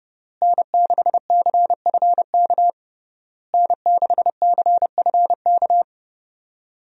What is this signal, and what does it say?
Morse code, heard as a keyed tone, about 30 wpm, 700 Hz. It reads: N6CFK N6CFK
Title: USA Call Signs - 5 Characters - Repeated 3x - 30wpm